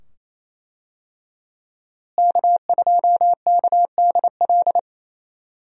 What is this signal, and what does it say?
Morse code, heard as a keyed tone, about 27 wpm, 700 Hz. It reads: K2KDL
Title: USA Call Signs - 5 Characters - Repeated 2x - 28wpm